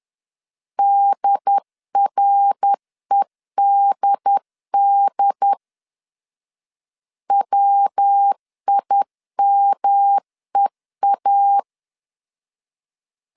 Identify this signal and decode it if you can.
Morse code, heard as a keyed tone, about 10 wpm, 800 Hz. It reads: DREDD WIMEA
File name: 10wpm